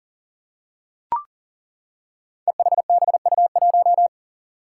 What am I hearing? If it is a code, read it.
Morse code, heard as a keyed tone, about 38 wpm, 700 Hz. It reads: EHBU1
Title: Five Challenging Letter-Number Combinations 40wpm